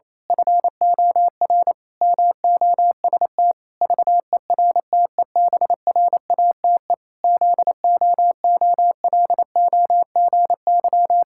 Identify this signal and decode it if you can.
Morse code, heard as a keyed tone, about 28 wpm, 700 Hz. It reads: FOR MOST VERTEBRATE ZOOLOGY
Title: Pangram Challenge v2 - 28wpm